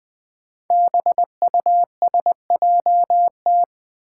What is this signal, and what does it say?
Morse code, heard as a keyed tone, about 20 wpm, 700 Hz. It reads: BUSJT
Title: Five Challenging Letter-Number Combinations 20wpm